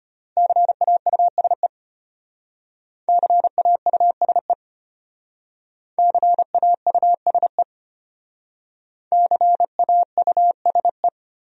Morse code, heard as a keyed tone, about 33 wpm, 700 Hz. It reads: CAUSE CAUSE CAUSE CAUSE